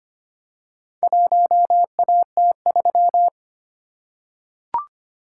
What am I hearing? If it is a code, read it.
Morse code, heard as a keyed tone, about 25 wpm, 700 Hz. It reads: 1AT3
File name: practice / 25wpm / Four Letter-Number Combinations - 25wpm